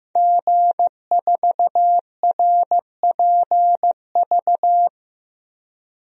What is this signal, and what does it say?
Morse code, heard as a keyed tone, about 15 wpm, 700 Hz. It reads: G4RPV
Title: DXCC-100 Call Signs - 5 Characters - Repeated 3x - 15wpm